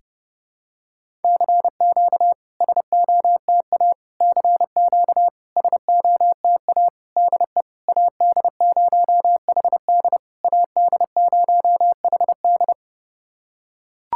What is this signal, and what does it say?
Morse code, heard as a keyed tone, about 30 wpm, 700 Hz. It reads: CQ SOTA CQ SOTA DE AD0HD AD0HD
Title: SOTA QSO Practice 30wpm